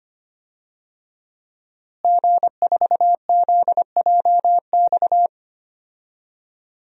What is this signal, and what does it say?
Morse code, heard as a keyed tone, about 25 wpm, 700 Hz. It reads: G4ZJX
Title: DXCC-100 Call Signs - 5 Characters - Repeated 3x - 25wpm